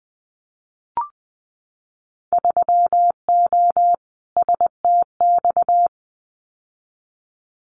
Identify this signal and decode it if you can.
Morse code, heard as a keyed tone, about 19 wpm, 700 Hz. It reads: 3O STX
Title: Winter Field Day Exchanges 20wpm